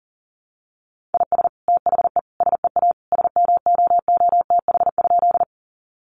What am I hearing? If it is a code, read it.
Morse code, heard as a keyed tone, about 37 wpm, 700 Hz. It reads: IS THE SEA SMOOTH?
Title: Sentences from Top 1400 Words 40wpm